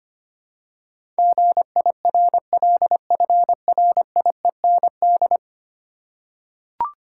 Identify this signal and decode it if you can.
Morse code, heard as a keyed tone, about 25 wpm, 700 Hz. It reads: GIRLFRIEND
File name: 10 Letter Words - First 3 Letters Given - 25wpm